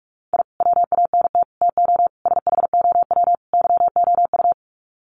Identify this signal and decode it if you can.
Morse code, heard as a keyed tone, about 44 wpm, 700 Hz. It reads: I WANT TO SHOW YOU
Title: Sets of 5 Words 45wpm